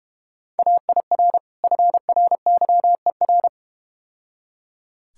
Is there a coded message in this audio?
Yes: Morse code from 0.6 to 3.5 s, heard as a keyed tone, about 31 wpm, 690 Hz. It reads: AIR FRYER